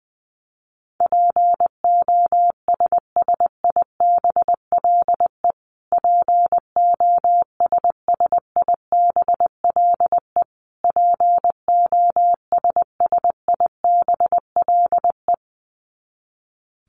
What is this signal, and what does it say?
Morse code, heard as a keyed tone, about 19 wpm, 700 Hz. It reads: POSSIBLE POSSIBLE POSSIBLE
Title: Top 500 Words Repeated 20wpm